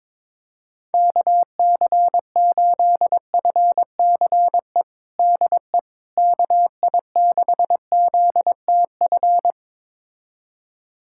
Morse code, heard as a keyed tone, about 22 wpm, 700 Hz. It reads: KC8FCE DE KI6ZTF
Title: USA Callsign DE Callsign - 6 Characters - 22wpm